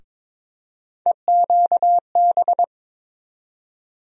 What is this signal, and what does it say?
Morse code, heard as a keyed tone, about 21 wpm, 700 Hz. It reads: EQB
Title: USA Call Sign Postfixes - 3 Letters 22wpm